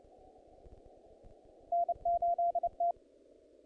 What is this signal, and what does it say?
Morse code, heard as a keyed tone, about 29 wpm, 690 Hz. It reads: N8T
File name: contests / naqp-cw-k5tr-op-k5pi- -1800